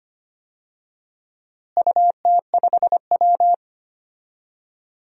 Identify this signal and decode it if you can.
Morse code, heard as a keyed tone, about 25 wpm, 700 Hz. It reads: UT5W
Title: DXCC-100 Call Signs - 4 Characters - Repeated 2x - 25wpm